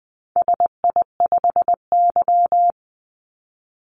Morse code, heard as a keyed tone, about 20 wpm, 700 Hz. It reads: SI5Y
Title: Call Signs - 20wpm